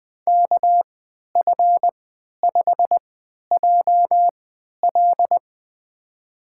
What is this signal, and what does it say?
Morse code, heard as a keyed tone, about 21 wpm, spaced as for 12 wpm, 700 Hz. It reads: KF5JL
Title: USA Call Signs - 5 Characters - 3x Character Spacing - 20wpm